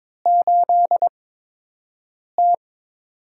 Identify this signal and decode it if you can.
Morse code, heard as a keyed tone, about 21 wpm, spaced as for 6 wpm, 700 Hz. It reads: 8T
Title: USA Call Signs - 4 Characters - 8x Character Spacing - 22wpm